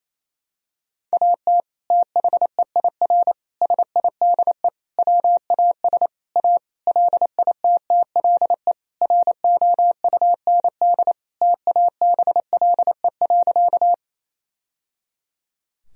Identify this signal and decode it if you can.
Morse code, heard as a keyed tone, about 28 wpm, 700 Hz. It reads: AT THEIR SIDE WAS A LITTLE ROUND TABLE.
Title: Sentences from Top 1000 Words - Review Part 2 - 28wpm